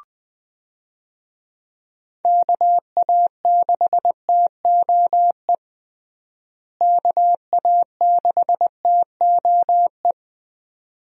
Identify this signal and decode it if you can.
Morse code, heard as a keyed tone, about 21 wpm, 700 Hz. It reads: KA6TOE KA6TOE
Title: USA Call Signs - 6 Characters - Repeated 2x - 20wpm